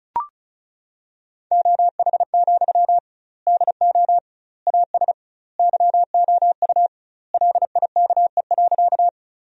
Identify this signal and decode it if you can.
Morse code, heard as a keyed tone, about 34 wpm, 700 Hz. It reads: OH, DO AS YOU LIKE.
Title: 2x Word-Spacing - Sentences from Top 500 Words - 35wpm